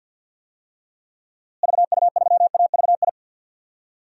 Morse code, heard as a keyed tone, about 48 wpm, 700 Hz. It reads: VU3AVI
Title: DXCC-100 Call Signs - 6 Characters - Repeated 2x - 50wpm